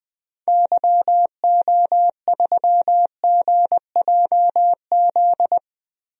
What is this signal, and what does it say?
Morse code, heard as a keyed tone, about 20 wpm, 700 Hz. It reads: YO3GJZ
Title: DXCC-100 Call Signs - 6 Characters - 20wpm